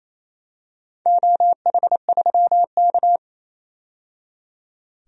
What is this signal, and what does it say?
Morse code, heard as a keyed tone, about 28 wpm, 700 Hz. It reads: OH3K
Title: DXCC-100 Call Signs - 4 Characters - 28wpm